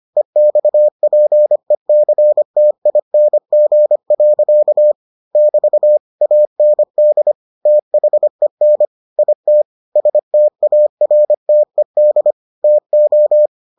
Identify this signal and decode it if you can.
Morse code, heard as a keyed tone, about 25 wpm, 600 Hz. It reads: EXPECTING. <BT> AND THEN IT STARTED TO